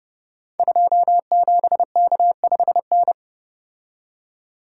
Morse code, heard as a keyed tone, about 31 wpm, 700 Hz. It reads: 27K5N
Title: Code Groups - 5 Letter:Numbers - 30wpm